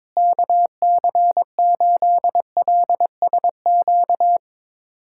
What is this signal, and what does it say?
Morse code, heard as a keyed tone, about 22 wpm, 700 Hz. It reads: KC8LSQ
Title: USA Callsign DE Callsign - 6 Characters - 22wpm